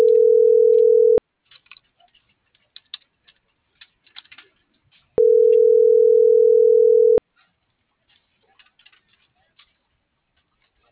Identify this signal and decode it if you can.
Ringback tone (North America), 440 + 480 Hz.